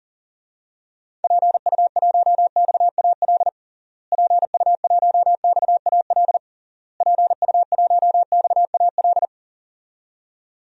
Morse code, heard as a keyed tone, about 40 wpm, 700 Hz. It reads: PU1XAL PU1XAL PU1XAL
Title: DXCC-100 Call Signs - 6 Characters - Repeated 3x - 40wpm